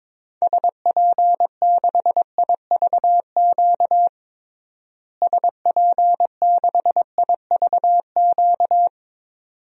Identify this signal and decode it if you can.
Morse code, heard as a keyed tone, about 22 wpm, 700 Hz. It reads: SP6IVQ SP6IVQ
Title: DXCC-100 Call Signs - 6 Characters - Repeated 2x - 22wpm